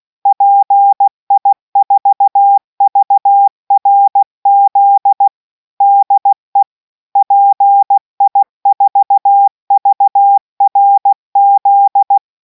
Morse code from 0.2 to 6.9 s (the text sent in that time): PI4VRZ DE